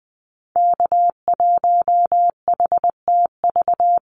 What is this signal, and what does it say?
Morse code, heard as a keyed tone, about 20 wpm, 700 Hz. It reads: K1HTV
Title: Call Signs - 20wpm